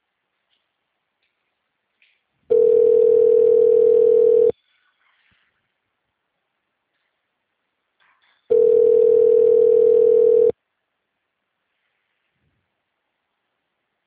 Ringback tone (North America), 440 + 480 Hz.
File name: out- -2016- - - .249